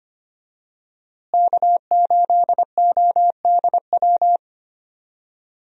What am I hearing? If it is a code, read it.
Morse code, heard as a keyed tone, about 25 wpm, 700 Hz. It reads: K8ODW
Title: USA Call Signs - 5 Characters - Repeated 3x - 25wpm